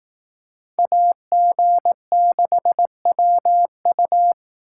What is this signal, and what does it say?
Morse code, heard as a keyed tone, about 19 wpm, 700 Hz. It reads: AG6WU
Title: Speed Racing - USA Call Signs 15wpm